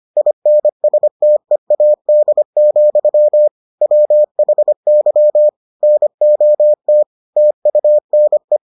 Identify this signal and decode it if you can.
Morse code, heard as a keyed tone, about 25 wpm, 600 Hz. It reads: INSTEAD, WHY NOT TUNE